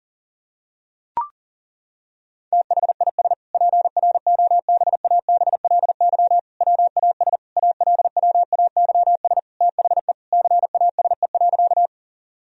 Morse code, heard as a keyed tone, about 37 wpm, 700 Hz. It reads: THIS PROBABLY WAS ALWAYS THE CASE.
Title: Sentences from Top 1000 Words - Review Part 4 - 40wpm